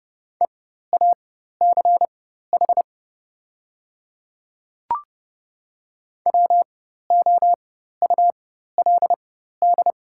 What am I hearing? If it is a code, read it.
Morse code, heard as a keyed tone, about 30 wpm, spaced as for 14 wpm, 700 Hz. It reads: EACH WOULD
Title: 4x Character Spacing - Top 100 Words - 30wpm